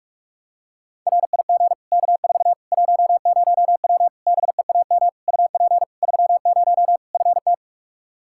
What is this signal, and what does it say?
Morse code, heard as a keyed tone, about 43 wpm, 700 Hz. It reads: RIG K4 10W BEAM UP 30 FT
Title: K1USN SST QSO Practice 45wpm